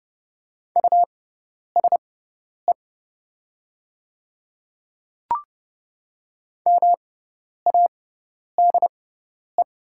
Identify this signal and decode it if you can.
Morse code, heard as a keyed tone, about 30 wpm, spaced as for 10 wpm, 700 Hz. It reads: USE MADE
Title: 6x Character Spacing - Top 200 Words - 30wpm